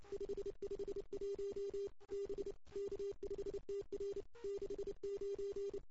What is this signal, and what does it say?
Morse code, heard as a keyed tone, about 26 wpm, 410 Hz. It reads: 551 B K5TR 69
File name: CW / 2008-ss-cw-k5tr-op-n5rz -02:00